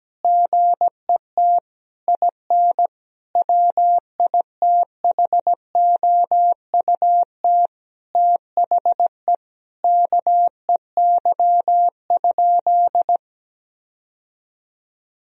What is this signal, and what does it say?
Morse code, heard as a keyed tone, about 17 wpm, 700 Hz. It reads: GET IN WITHOUT THE KEY?
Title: Sentences from Top 1000 Words - Review Part 4 - 17wpm